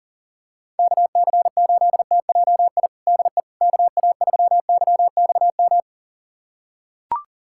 Morse code, heard as a keyed tone, about 40 wpm, 700 Hz. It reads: KC8TJI DE KA3YXM
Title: USA Callsign DE Callsign - 6 Characters - 40wpm